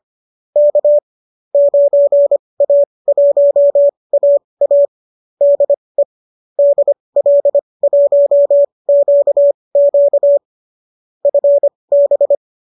Morse code, heard as a keyed tone, about 25 wpm, spaced as for 20 wpm, 570 Hz. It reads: K 9A1AA DE DL1QQ FB
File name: qso219 20